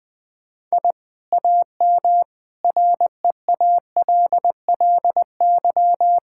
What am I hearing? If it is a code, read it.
Morse code, heard as a keyed tone, about 20 wpm, 700 Hz. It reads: I AM REALLY
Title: Sentences from Top 1000 Words - Review Part 6 - 20wpm